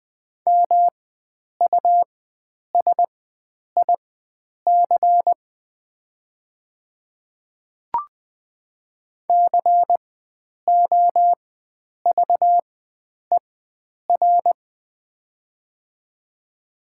Morse code, heard as a keyed tone, about 20 wpm, spaced as for 9 wpm, 700 Hz. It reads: MUSIC COVER
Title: 4x Character Spacing - Top 300 Words - 20wpm